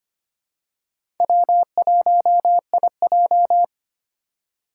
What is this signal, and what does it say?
Morse code, heard as a keyed tone, about 25 wpm, 700 Hz. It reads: W1IJ
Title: USA Call Signs - 4 Characters - 2x Character Spacing - 25wpm